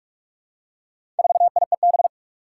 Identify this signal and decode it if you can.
Morse code, heard as a keyed tone, about 46 wpm, 700 Hz. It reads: 4IEB
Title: Four Challenging Letter-Number Combinations 45wpm